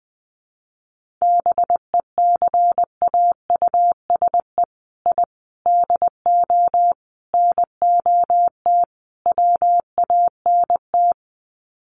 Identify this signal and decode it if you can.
Morse code, heard as a keyed tone, about 20 wpm, 700 Hz. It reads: BECAUSE I DO NOT WANT
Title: Sets of 5 Words Encore 20wpm